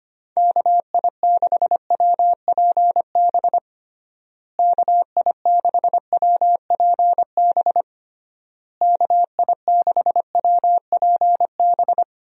Morse code, heard as a keyed tone, about 25 wpm, 700 Hz. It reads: KI6WPB KI6WPB KI6WPB
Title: USA Call Signs - 6 Characters - Repeated 3x - 25wpm